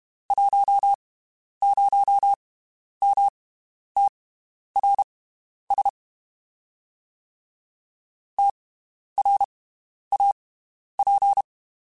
Morse code, heard as a keyed tone, about 32 wpm, spaced as for 11 wpm, 800 Hz. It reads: 10MTRS TRAP
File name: (spaced) / QSOs Spaced-mp3s / WD0FBT 1 @30 sp